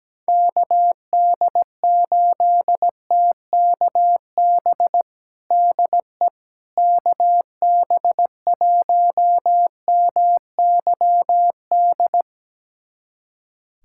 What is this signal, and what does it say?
Morse code, heard as a keyed tone, about 17 wpm, 700 Hz. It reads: KD8TKB DE KB1MYD
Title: USA Callsign DE Callsign - 6 Characters - 17wpm